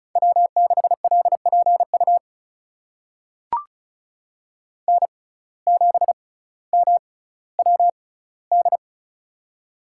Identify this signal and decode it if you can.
Morse code, heard as a keyed tone, about 36 wpm, spaced as for 18 wpm, 700 Hz. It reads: W6LPU N7MWD
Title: USA Call Signs - 5 Characters - 6x Character Spacing - 35wpm